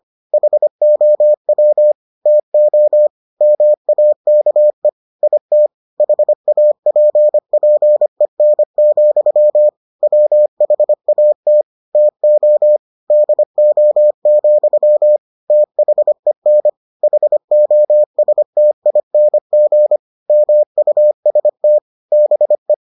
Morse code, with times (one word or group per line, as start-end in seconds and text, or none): 0.3-1.9 HOW
2.3-3.1 TO
3.4-4.9 MAKE
5.2-5.7 IT
6.0-9.7 HAPPEN,
10.0-11.6 WHAT
11.9-12.8 TO
13.1-15.2 DO,
15.5-16.7 THEN
17.0-20.0 HOSTING
20.3-21.8 MUST
22.1-22.7 BE